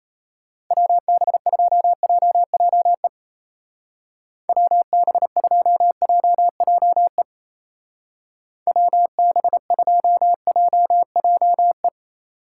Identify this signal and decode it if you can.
Morse code, heard as a keyed tone, about 34 wpm, 700 Hz. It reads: WB2JJE WB2JJE WB2JJE
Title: Speed Racing - USA 2x3 Call Signs 28wpm